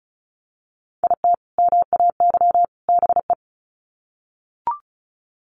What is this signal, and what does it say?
Morse code, heard as a keyed tone, about 34 wpm, 700 Hz. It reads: IT MAY BE
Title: Sets of 3 Words 35wpm